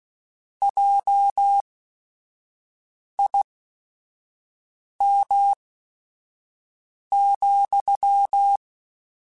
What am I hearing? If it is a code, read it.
Morse code, heard as a keyed tone, about 16 wpm, spaced as for 5 wpm, 800 Hz. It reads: JIM,
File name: QSOs / QSOs-mp3s / WD0BGP 1@5